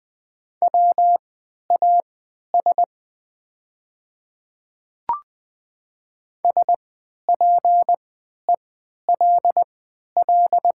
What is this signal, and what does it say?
Morse code, heard as a keyed tone, about 20 wpm, spaced as for 11 wpm, 700 Hz. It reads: WAS SPELL
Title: 3x Character Spacing - Top 200 Words - 20wpm